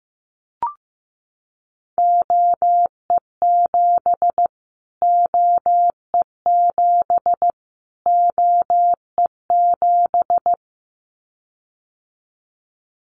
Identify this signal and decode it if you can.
Morse code, heard as a keyed tone, about 15 wpm, 700 Hz. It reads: OE7 OE7 OE7
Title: Callsign Prefixes for Top 25 DXCC - Repeated 3x - 15wpm